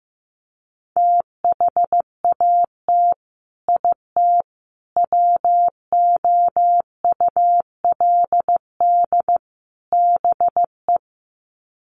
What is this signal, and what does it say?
Morse code, heard as a keyed tone, about 15 wpm, 700 Hz. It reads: THAT IT WOULD BE
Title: Sets of 4 Words 15wpm